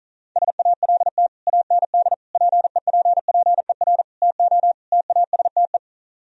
Morse code, heard as a keyed tone, about 45 wpm, 700 Hz. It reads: SALT AND PEPPER TO TASTE